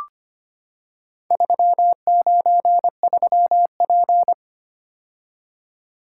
Morse code, heard as a keyed tone, about 25 wpm, 700 Hz. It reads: 393P
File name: practice / 25wpm / Four Letter-Number Combinations - 25wpm